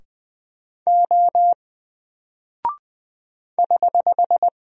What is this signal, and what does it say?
Morse code, heard as a keyed tone, about 20 wpm, 700 Hz. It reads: O <HH>
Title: Single Character (Letter:Number:Punctuation:Prosign) 20wpm